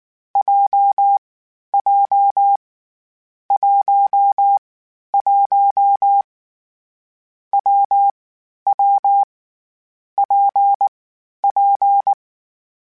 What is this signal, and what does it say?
Morse code, heard as a keyed tone, about 19 wpm, 800 Hz. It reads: J J 1 1 W W P P
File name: Variable Speed / Variable Speed-mp3s / WPJ WPJ1 sp @18